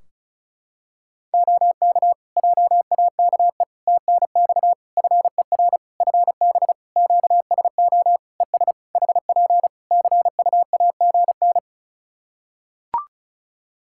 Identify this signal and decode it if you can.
Morse code, heard as a keyed tone, about 36 wpm, 700 Hz. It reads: OK JAKE TNX FER FB QSO ES HP CUAGN
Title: Practice QSO - Exchange 4 - Ending - 35wpm